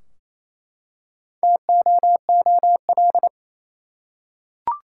Morse code, heard as a keyed tone, about 28 wpm, 700 Hz. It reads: TOOL
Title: Four Letter Words - 28wpm